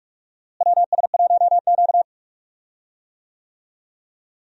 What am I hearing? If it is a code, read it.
Morse code, heard as a keyed tone, about 42 wpm, 700 Hz. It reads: WS1Q
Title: USA Call Signs - 4 Characters - Repeated 3x - 45wpm